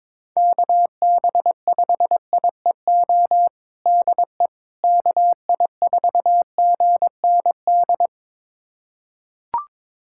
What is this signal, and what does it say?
Morse code, heard as a keyed tone, about 22 wpm, 700 Hz. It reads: KB5IEO DE KI4GND
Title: USA Callsign DE Callsign 22wpm